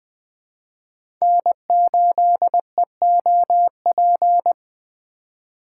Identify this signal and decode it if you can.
Morse code, heard as a keyed tone, about 20 wpm, 700 Hz. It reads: N8EOP
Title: USA Call Signs - 5 Characters - 2x Character Spacing - 20wpm